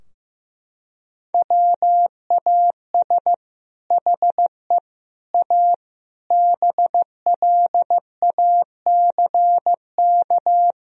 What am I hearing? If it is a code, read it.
Morse code, heard as a keyed tone, about 15 wpm, 700 Hz. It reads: WAS HE A BLACK
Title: Sentences from Top 1000 Words - Review Part 2 - 15wpm